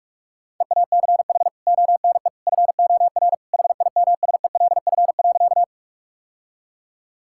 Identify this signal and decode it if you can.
Morse code, heard as a keyed tone, about 46 wpm, 700 Hz. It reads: EACH ONE FOR HIMSELF.
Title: Sentences from Top 1600 Words 45wpm